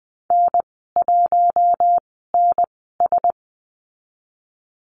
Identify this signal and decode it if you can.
Morse code, heard as a keyed tone, about 20 wpm, spaced as for 14 wpm, 700 Hz. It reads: N1NS
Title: 042 Lesson 8g - Number 1 - Call Signs 20wpm